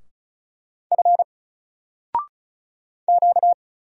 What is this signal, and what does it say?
Morse code, heard as a keyed tone, about 34 wpm, 700 Hz. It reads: F Q
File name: Single-Letter-Number-35wpm